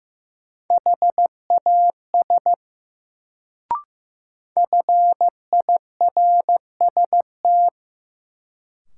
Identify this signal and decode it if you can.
Morse code, heard as a keyed tone, about 15 wpm, 700 Hz. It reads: HAS FIRST
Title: Sentences from Top 100 Words - New Word Drill - 15wpm